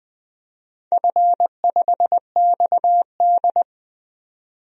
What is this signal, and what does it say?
Morse code, heard as a keyed tone, about 20 wpm, 700 Hz. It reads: F5XD
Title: DXCC-100 Call Signs - 4 Characters - 20wpm